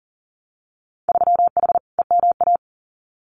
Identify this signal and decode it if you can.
Morse code, heard as a keyed tone, about 39 wpm, 700 Hz. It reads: 3H EMA
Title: Winter Field Day Exchanges Encore 40wpm